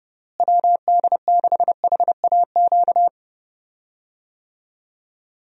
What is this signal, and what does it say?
Morse code, heard as a keyed tone, about 30 wpm, 700 Hz. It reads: WD6HAQ
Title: USA Call Signs - 6 Characters - Repeated 2x - 30wpm